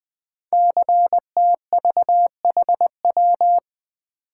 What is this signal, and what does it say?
Morse code, heard as a keyed tone, about 20 wpm, 700 Hz. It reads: CTVHW
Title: Code Groups - 5 Letters - 20wpm